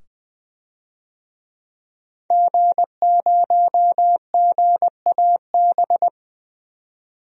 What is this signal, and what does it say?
Morse code, heard as a keyed tone, about 20 wpm, 700 Hz. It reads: G0GAB
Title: DXCC-100 Call Signs - 5 Characters - 20wpm